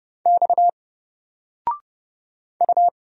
Morse code, heard as a keyed tone, about 30 wpm, 700 Hz. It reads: X U
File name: Single-Letter-Number-30wpm